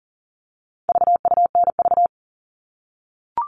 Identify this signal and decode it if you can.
Morse code, heard as a keyed tone, about 40 wpm, 700 Hz. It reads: VUNV
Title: All Cut Numbers - 4-Digit 40wpm